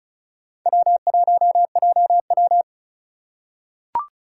Morse code, heard as a keyed tone, about 36 wpm, 700 Hz. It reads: W1JW
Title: USA Call Signs - 4 Characters - 3x Character Spacing - 35wpm